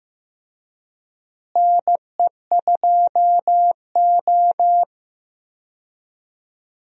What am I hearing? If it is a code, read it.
Morse code, heard as a keyed tone, about 15 wpm, 700 Hz. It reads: NE2O
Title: USA Call Signs - 4 Characters - Repeated 3x - 15wpm